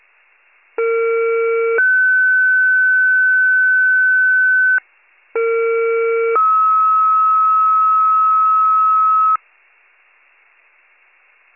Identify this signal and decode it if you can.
Two-tone page, heard as a one-second tone, then a three-second tone, about 460 then 1500 Hz, about 460 then 1290 Hz. It is heard from 0.8 to 9.4 s.